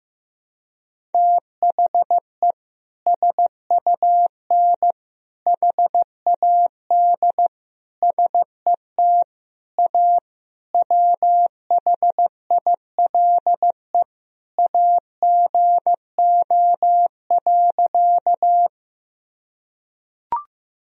Morse code, heard as a keyed tone, about 16 wpm, 700 Hz. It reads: THE SUN HAD SET A WHILE AGO.
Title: Sentences from Top 1000 Words - Review Part 1 - 15wpm